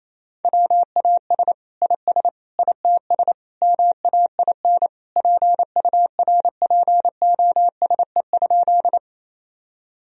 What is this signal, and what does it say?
Morse code, heard as a keyed tone, about 28 wpm, 700 Hz. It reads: WAS IS ITS MAIN PURPOSE?
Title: Sentences from Top 1000 Words - Review Part 2 - 28wpm